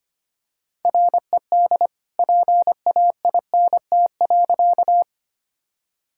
Morse code, heard as a keyed tone, about 25 wpm, 700 Hz. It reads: RED PAINT.
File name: practice / 25wpm / All-Age Appropriate Jokes - 25wpm